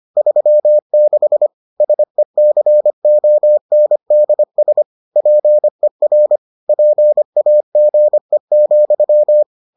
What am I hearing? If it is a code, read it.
Morse code, heard as a keyed tone, about 25 wpm, 600 Hz. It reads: 36 SECONDS PER PAGE,